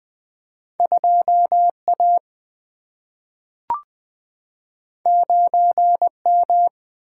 Morse code, heard as a keyed tone, about 20 wpm, 700 Hz. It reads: 2 A 9 M
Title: Single-Letter Single-Number 20wpm